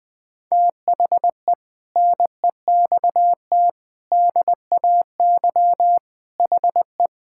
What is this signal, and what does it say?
Morse code, heard as a keyed tone, about 20 wpm, 700 Hz. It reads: THE NEXT DAY HE
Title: Sentences with 3-Word Phrases - 20wpm